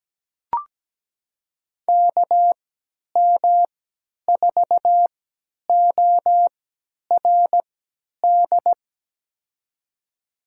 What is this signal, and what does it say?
Morse code, heard as a keyed tone, about 17 wpm, spaced as for 10 wpm, 700 Hz. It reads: KM4ORD
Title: USA Call Signs - 6 Characters - 3x Character Spacing - 17wpm